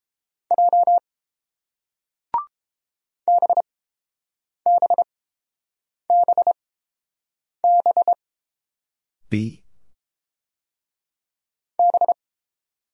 Morse code, heard as a keyed tone, about 30 wpm, spaced as for 7 wpm, 700 Hz. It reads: J BBBB B